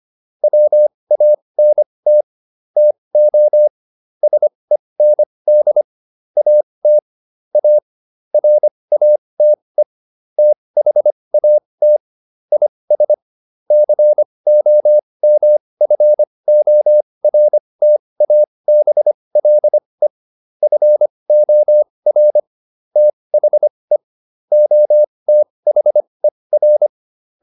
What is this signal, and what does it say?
Morse code, heard as a keyed tone, about 25 wpm, spaced as for 20 wpm, 600 Hz. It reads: WANT TO SEND AT A RATE THAT IS COMFORTABLE FOR THE OTHER